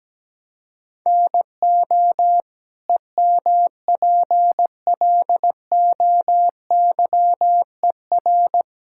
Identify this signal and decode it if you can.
Morse code, heard as a keyed tone, about 17 wpm, 700 Hz. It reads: NO EMPLOYER
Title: Sentences from Top 1600 Words 17wpm